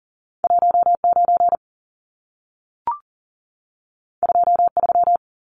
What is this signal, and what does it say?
Morse code, heard as a keyed tone, about 40 wpm, 700 Hz. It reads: 19 23
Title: Numbers - 2-Digit 40wpm